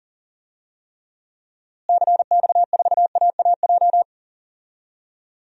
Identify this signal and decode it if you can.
Morse code, heard as a keyed tone, about 40 wpm, 700 Hz. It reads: CX4AAJ
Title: DXCC-100 Call Signs - 6 Characters - Repeated 3x - 40wpm